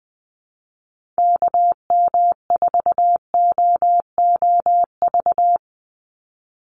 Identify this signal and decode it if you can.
Morse code, heard as a keyed tone, about 20 wpm, 700 Hz. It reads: KM4OOV
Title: USA Call Signs 20wpm